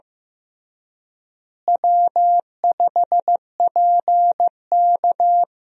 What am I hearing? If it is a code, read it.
Morse code, heard as a keyed tone, about 15 wpm, 700 Hz. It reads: W5PK
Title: USA Call Signs - 4 Characters - Repeated 3x - 15wpm